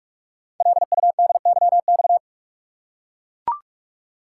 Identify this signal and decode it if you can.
Morse code, heard as a keyed tone, about 44 wpm, 690 Hz. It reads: LUDYX
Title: Code Groups - 5 Letters - 45wpm